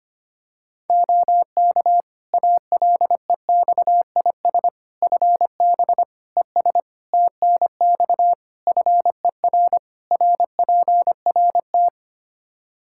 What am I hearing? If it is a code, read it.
Morse code, heard as a keyed tone, about 25 wpm, 700 Hz. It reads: OK ALEXIS FB ES TNX FER RPRT
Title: Practice QSO - Exchange 2 - Info - 25wpm